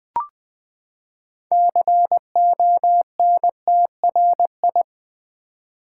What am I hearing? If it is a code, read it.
Morse code, heard as a keyed tone, about 20 wpm, 700 Hz. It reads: CONTRI
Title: Build A Word - 6 Letters At A Time - 12-Letter Words - 20wpm